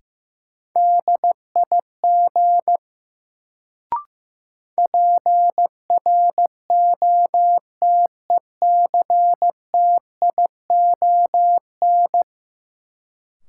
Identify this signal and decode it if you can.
Morse code, heard as a keyed tone, about 16 wpm, 700 Hz. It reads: DIG PROTECTION
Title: Sentences from Top 1600 Words - New Word Drill - 15wpm